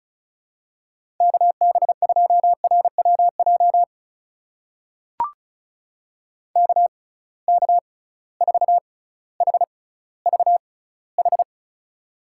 Morse code, heard as a keyed tone, about 36 wpm, spaced as for 18 wpm, 700 Hz. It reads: KB2RWJ KK4HVH
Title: USA Call Signs - 6x Character Spacing - 35wpm